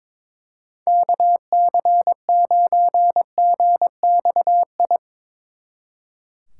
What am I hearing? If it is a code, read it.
Morse code, heard as a keyed tone, about 22 wpm, 700 Hz. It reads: KC9GXI
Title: Speed Racing - USA Call Signs 22wpm